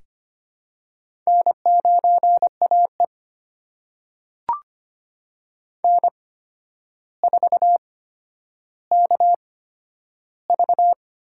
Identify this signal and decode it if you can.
Morse code, heard as a keyed tone, about 25 wpm, spaced as for 11 wpm, 700 Hz. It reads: N9AE N4KV
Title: USA Call Signs - 4 Characters - 8x Character Spacing - 25wpm